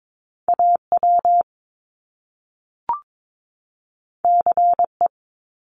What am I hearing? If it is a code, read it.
Morse code, heard as a keyed tone, about 22 wpm, 700 Hz. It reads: AW CE
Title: Top Two Letter Combinations in Words 22wpm